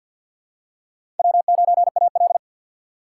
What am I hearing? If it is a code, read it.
Morse code, heard as a keyed tone, about 48 wpm, 700 Hz. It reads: W9AL
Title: USA Call Signs - 4 Characters - 3x Character Spacing - 50wpm